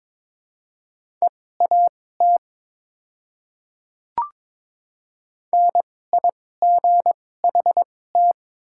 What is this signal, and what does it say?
Morse code, heard as a keyed tone, about 22 wpm, spaced as for 16 wpm, 690 Hz. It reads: EAT NIGHT
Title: 2x Character Spacing - Top 500 Words - 22wpm